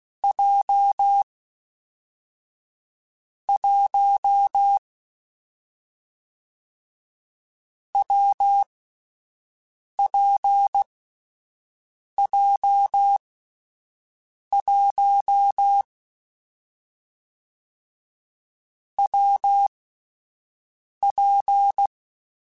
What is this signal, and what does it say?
Morse code, heard as a keyed tone, about 16 wpm, 800 Hz. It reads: J 1 W P J 1 W P
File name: Difficult Combos / Difficult Combos-mp3s / WPJ WPJ1 @10